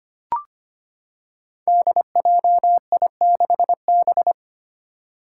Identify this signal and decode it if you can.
Morse code, heard as a keyed tone, about 25 wpm, 700 Hz. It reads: DJI6B
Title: Five Challenging Letter-Number Combinations 25wpm